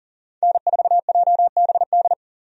Morse code, heard as a keyed tone, about 40 wpm, 700 Hz. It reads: N4JBD
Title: USA Call Signs - 5 Characters - 40wpm